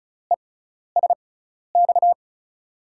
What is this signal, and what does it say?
Morse code, heard as a keyed tone, about 34 wpm, spaced as for 12 wpm, 700 Hz. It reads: ESX
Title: USA Call Signs - 6x Character Spacing - 35wpm